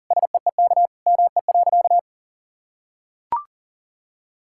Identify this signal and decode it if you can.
Morse code, heard as a keyed tone, about 37 wpm, 700 Hz. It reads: SEEK ME.
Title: Sentences from Top 1600 Words 40wpm